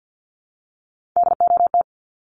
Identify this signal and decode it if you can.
Morse code, heard as a keyed tone, about 50 wpm, 700 Hz. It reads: DOT
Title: Top 1251-1500 Words 50wpm